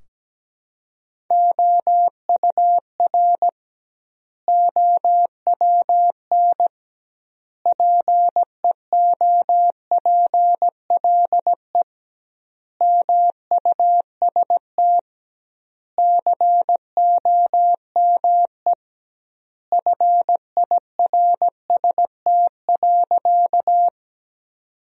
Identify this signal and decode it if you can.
Morse code, heard as a keyed tone, about 17 wpm, 700 Hz. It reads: OUR OWN PEOPLE MUST COME FIRST.
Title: 2x Word-Spacing - Sentences from Top 300 Words - 17wpm